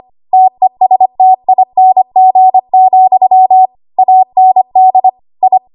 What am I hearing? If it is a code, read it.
Morse code, heard as a keyed tone, about 25 wpm, 750 Hz. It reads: TESTING, AND I
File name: Archive / 25 WPM / 25WPM